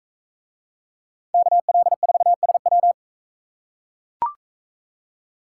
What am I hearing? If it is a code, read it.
Morse code, heard as a keyed tone, about 42 wpm, 700 Hz. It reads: KL4SW